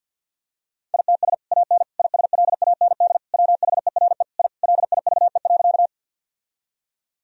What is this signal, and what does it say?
Morse code, heard as a keyed tone, about 50 wpm, 700 Hz. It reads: ITS AN ISLAND WHERE I LIVE.
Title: Sentences from Top 1000 Words - Review Part 3 - 50wpm